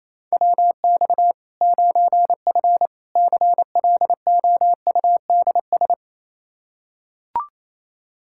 Morse code, heard as a keyed tone, about 28 wpm, 700 Hz. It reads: WX 9F CLOUDS
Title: Practice QSO - Exchange 2 - Info - 28wpm